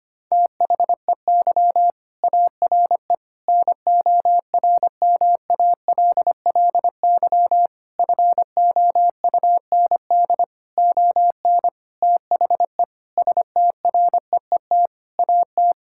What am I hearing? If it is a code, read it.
Morse code, heard as a keyed tone, about 25 wpm, 700 Hz. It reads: THEY ARE NORMALLY FOUND ON THE STREET AT
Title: Sentences from Top 1000 Words - Review Part 6 - 25wpm